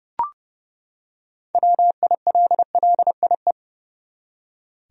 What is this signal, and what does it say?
Morse code, heard as a keyed tone, about 30 wpm, 700 Hz. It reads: WILLIE
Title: USA Common Names - 30wpm